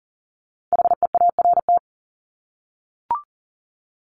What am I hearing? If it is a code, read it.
Morse code, heard as a keyed tone, about 40 wpm, 700 Hz. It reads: HEART
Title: Top 1001-1250 Words 40wpm